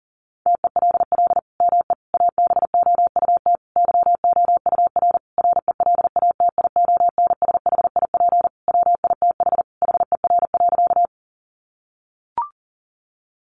Morse code, heard as a keyed tone, about 37 wpm, 700 Hz. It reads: TELL ME ABOUT YOUR RELATIONSHIP WITH HER.
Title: Sentences from Top 1100 Words 40wpm